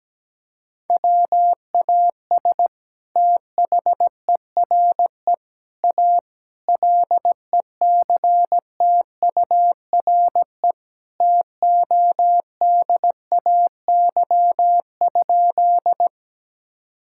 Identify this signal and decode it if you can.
Morse code, heard as a keyed tone, about 17 wpm, 700 Hz. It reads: WAS THERE A LECTURE TODAY?
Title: Sentences from Top 1600 Words 17wpm